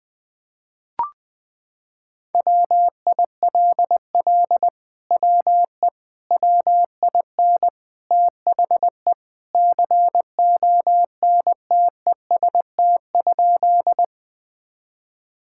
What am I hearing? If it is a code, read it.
Morse code, heard as a keyed tone, about 20 wpm, 700 Hz. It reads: WILL WE WIN THE CONTEST?
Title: Sentences from Top 1600 Words 20wpm